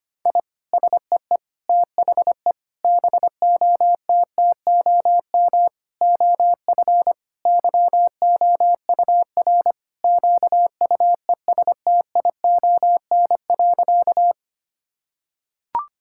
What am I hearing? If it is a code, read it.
Morse code, heard as a keyed tone, about 25 wpm, 700 Hz. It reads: I SEE THE BOTTOM OF YOUR QUESTION.
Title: Sentences from Top 1000 Words - Review Part 6 - 25wpm